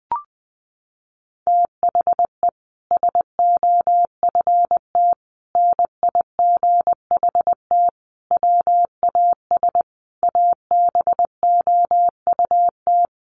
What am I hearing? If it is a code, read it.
Morse code, heard as a keyed tone, about 20 wpm, 700 Hz. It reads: THE SOFT NIGHT WAS ABOUT
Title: Sentences from Top 1500 Words 20wpm